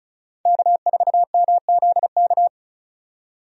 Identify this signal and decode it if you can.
Morse code, heard as a keyed tone, about 34 wpm, 700 Hz. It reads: K4MZK
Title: USA Call Signs - 5 Characters - 35wpm